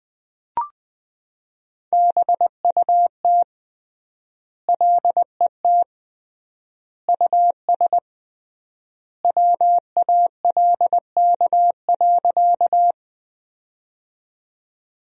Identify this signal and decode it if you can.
Morse code, heard as a keyed tone, about 20 wpm, 700 Hz. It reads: BUT LET US WALK.
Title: 3x Word-Spacing - Sentences from Top 300 Words - 20wpm